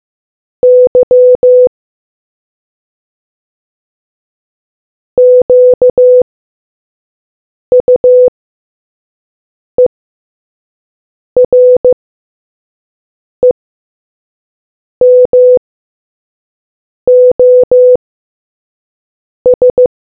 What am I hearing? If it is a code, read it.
Morse code, heard as a keyed tone, about 15 wpm, spaced as for 5 wpm, 500 Hz. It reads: Y QUEREMOS